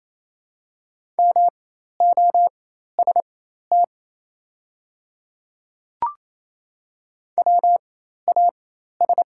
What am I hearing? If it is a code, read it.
Morse code, heard as a keyed tone, about 28 wpm, spaced as for 13 wpm, 700 Hz. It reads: MOST WAS
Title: 4x Character Spacing - Top 100 Words - 28wpm